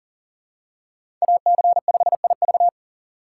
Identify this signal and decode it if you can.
Morse code, heard as a keyed tone, about 40 wpm, 700 Hz. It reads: AC5IV
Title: USA Call Signs - 5 Characters - 2x Character Spacing - 40wpm